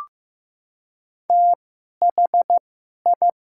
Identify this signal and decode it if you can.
Morse code, heard as a keyed tone, about 15 wpm, spaced as for 11 wpm, 700 Hz. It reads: THI
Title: 2x Character Spacing - Top 100 Words - 15wpm